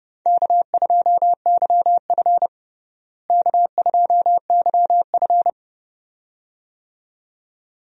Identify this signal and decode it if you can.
Morse code, heard as a keyed tone, about 30 wpm, 700 Hz. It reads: K2YF K2YF
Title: USA Call Signs - 4 Characters - Repeated 2x - 30wpm